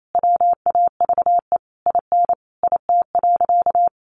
Morse code, heard as a keyed tone, about 28 wpm, 700 Hz. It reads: WAVE IN IT.
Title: Sentences from Top 1400 Words 28wpm